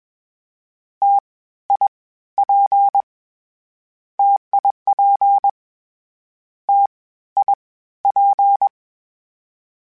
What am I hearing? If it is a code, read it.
Morse code, heard as a keyed tone, about 21 wpm, 800 Hz. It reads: T I P TIP T I P
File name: Variable Spacing / Variable Spacing-mp3s / T AOEs @20